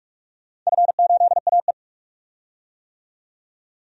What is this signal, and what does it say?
Morse code, heard as a keyed tone, about 48 wpm, 700 Hz. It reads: F8AE
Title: DXCC-100 Call Signs - 4 Characters - 45wpm